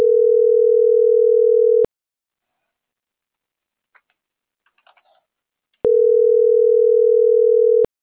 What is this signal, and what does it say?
Ringback tone (North America), 440 + 480 Hz.
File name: CENTRAL MABE / out karen.maya